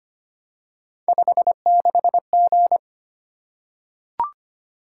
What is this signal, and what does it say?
Morse code, heard as a keyed tone, about 25 wpm, 700 Hz. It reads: 56G
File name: Three Letter-Number Combinations - 25wpm